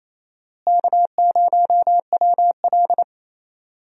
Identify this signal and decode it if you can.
Morse code, heard as a keyed tone, about 28 wpm, 700 Hz. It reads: K0WL
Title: USA Call Signs - 4 Characters - Repeated 3x - 28wpm